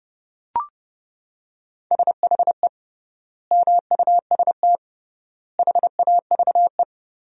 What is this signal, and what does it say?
Morse code, heard as a keyed tone, about 30 wpm, 700 Hz. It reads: SHE MUST HAVE
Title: 3x Word-Spacing - Sentences from Top 300 Words - 30wpm